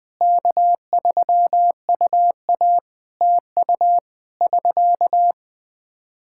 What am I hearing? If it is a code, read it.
Morse code, heard as a keyed tone, about 21 wpm, 700 Hz. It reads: K3UA TU <SK>
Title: Practice QSO - Exchange 4 - Ending - 20wpm